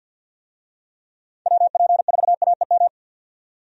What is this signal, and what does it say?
Morse code, heard as a keyed tone, about 51 wpm, 700 Hz. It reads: WP4AEM
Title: USA Call Signs - 6 Characters - 50wpm